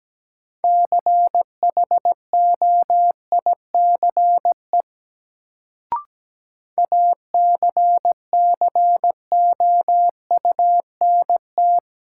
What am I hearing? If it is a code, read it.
Morse code, heard as a keyed tone, about 17 wpm, 700 Hz. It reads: CHOICE ACCOUNT
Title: Sentences from Top 900 Words - New Word Drill - 17wpm